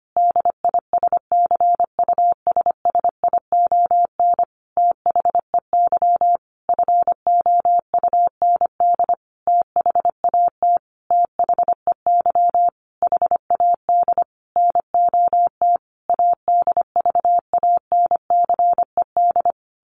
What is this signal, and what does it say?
Morse code, heard as a keyed tone, about 25 wpm, 700 Hz. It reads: DISCUSSION THEY FOUND THAT THEY HAD NOT ADVANCED
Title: Sentences from Top 1200 Words 25wpm